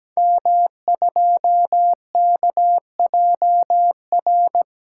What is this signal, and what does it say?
Morse code, heard as a keyed tone, about 17 wpm, 700 Hz. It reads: M2KJR
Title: Code Groups - 5 Letter:Numbers - 17wpm